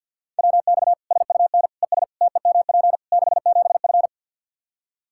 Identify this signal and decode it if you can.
Morse code, heard as a keyed tone, about 50 wpm, 700 Hz. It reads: WX SUN ES TEMP 67F
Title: Practice QSO - Exchange 2 - Info - 50wpm